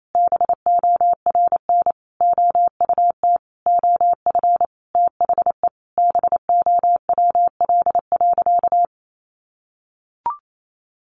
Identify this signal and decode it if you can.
Morse code, heard as a keyed tone, about 28 wpm, 700 Hz. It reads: BORN OUT OF THE BOWL.
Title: Sentences from Top 1200 Words 28wpm